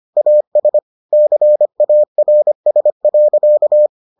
Morse code, heard as a keyed tone, about 25 wpm, 600 Hz. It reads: AS CARS.